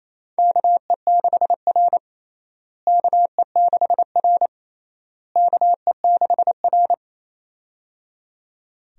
Morse code, heard as a keyed tone, about 28 wpm, 700 Hz. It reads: KE6R KE6R KE6R
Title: USA Call Signs - 4 Characters - Repeated 3x - 28wpm